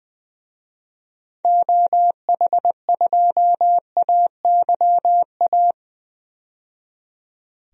Morse code, heard as a keyed tone, about 20 wpm, 700 Hz. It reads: OH2AYA
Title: DXCC-100 Call Signs - 6 Characters - 20wpm